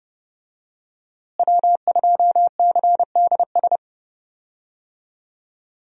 Morse code, heard as a keyed tone, about 30 wpm, 700 Hz. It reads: W2CDS
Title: USA Call Signs - 5 Characters - Repeated 3x - 30wpm